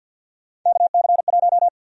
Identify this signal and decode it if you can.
Morse code, heard as a keyed tone, about 49 wpm, 700 Hz. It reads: KC1
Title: USA Call Sign Prefixes - 50wpm